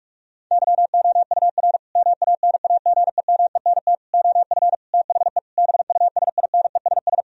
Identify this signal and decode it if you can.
Morse code, heard as a keyed tone, about 46 wpm, 700 Hz. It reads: YOUR MANAGEMENT OF THE BUSINESS
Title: Sentences from Top 1000 Words - Review Part 4 - 45wpm